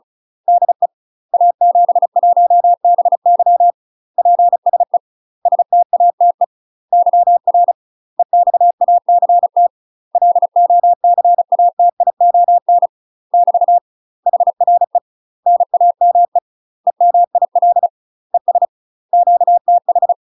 Morse code, heard as a keyed tone, about 36 wpm, 700 Hz. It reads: DE A71BY PSE STATE YR EXACT LOCATION <BT> HRE NAME EMIL ES QTH
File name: UK MASTER QSOs TRAINING / 35WPM / UK MASTER QSOs 0020